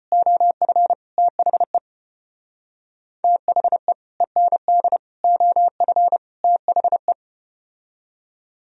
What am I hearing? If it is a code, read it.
Morse code, heard as a keyed tone, about 29 wpm, 700 Hz. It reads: OF THE THE END OF THE
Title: Speed-Racing - Sets of 4 Words 25wpm